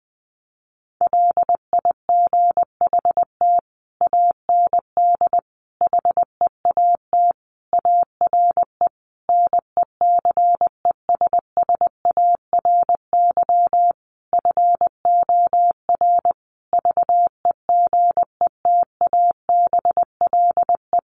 Morse code, heard as a keyed tone, about 20 wpm, 700 Hz. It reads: LIGHT AND HEAT ARE NECESSARY FOR VEGETABLE
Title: Sentences from Top 1300 Words 20wpm